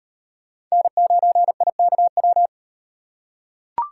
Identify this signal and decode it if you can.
Morse code, heard as a keyed tone, about 34 wpm, 700 Hz. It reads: N9IKW